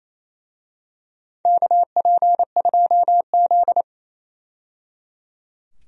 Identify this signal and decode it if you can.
Morse code, heard as a keyed tone, about 28 wpm, 700 Hz. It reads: KP2Z
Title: DXCC-100 Call Signs - 4 Characters - 28wpm